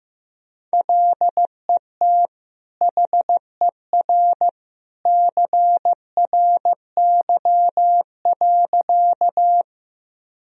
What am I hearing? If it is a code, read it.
Morse code, heard as a keyed tone, about 15 wpm, 700 Hz. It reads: LET HER CRY.
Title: Sentences from Top 1000 Words - Review Part 2 - 15wpm